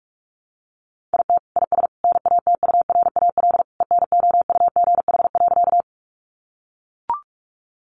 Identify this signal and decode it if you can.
Morse code, heard as a keyed tone, about 44 wpm, 700 Hz. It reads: IT IS NATURAL ENOUGH.
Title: Sentences from Top 700 Words 45wpm